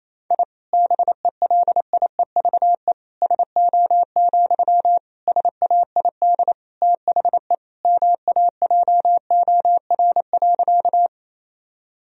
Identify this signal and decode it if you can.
Morse code, heard as a keyed tone, about 27 wpm, 700 Hz. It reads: I BELIEVE SO, SAID THE MAJOR.
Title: Sentences from Top 1000 Words - Review Part 4 - 28wpm